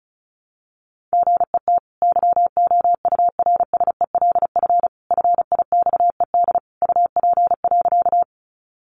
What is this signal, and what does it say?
Morse code, heard as a keyed tone, about 34 wpm, 700 Hz. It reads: GET YOURSELF FIXED UP.
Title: Sentences from Top 1100 Words 35wpm